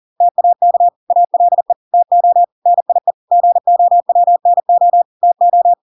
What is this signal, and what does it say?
Morse code, heard as a keyed tone, about 40 wpm, 700 Hz. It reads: TAK ALE TO NIE GOWNO TO